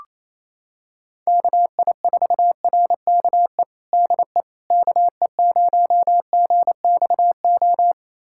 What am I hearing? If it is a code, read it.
Morse code, heard as a keyed tone, about 28 wpm, 700 Hz. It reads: KI4RKE DE KE0GXO
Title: USA Callsign DE Callsign - 6 Characters - 28wpm